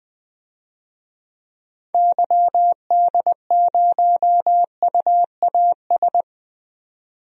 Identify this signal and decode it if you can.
Morse code, heard as a keyed tone, about 20 wpm, 700 Hz. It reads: YD0UAS
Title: DXCC-100 Call Signs - 6 Characters - 20wpm